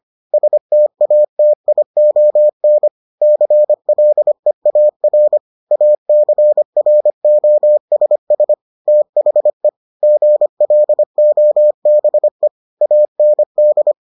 Morse code, heard as a keyed tone, about 25 wpm, 600 Hz. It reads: STATION CLEAR ACROSS THE GLOBE AND